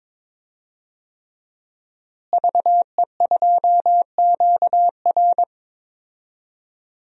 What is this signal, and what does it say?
Morse code, heard as a keyed tone, about 22 wpm, 700 Hz. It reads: VE2QR
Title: DXCC-100 Call Signs - 5 Characters - Repeated 3x - 22wpm